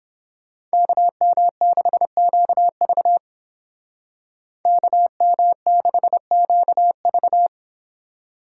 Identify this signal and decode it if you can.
Morse code, heard as a keyed tone, about 29 wpm, 700 Hz. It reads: KM6QV KM6QV